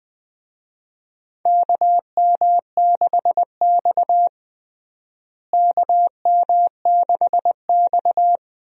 Morse code, heard as a keyed tone, about 20 wpm, 700 Hz. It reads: KM6X KM6X
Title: USA Call Signs - 4 Characters - Repeated 3x - 20wpm